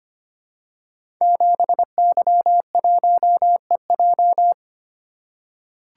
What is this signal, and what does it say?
Morse code, heard as a keyed tone, about 25 wpm, 700 Hz. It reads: 7Y1EJ
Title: Code Groups - 5 Letter:Numbers - 25wpm